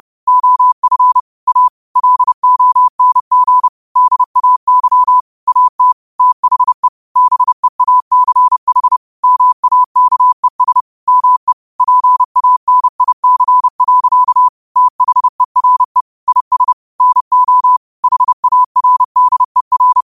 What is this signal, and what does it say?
Morse code, heard as a keyed tone, about 30 wpm, 1000 Hz. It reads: OF A LONG DAY AT THE BEACH MAKES ME PANIC. THERE IS NO HARDER
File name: qotdfiles / 30wpm